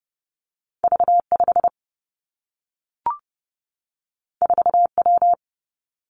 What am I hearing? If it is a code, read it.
Morse code, heard as a keyed tone, about 30 wpm, 700 Hz. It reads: V 5 4 W
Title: Two Challenging Letter-Number Combinations 30wpm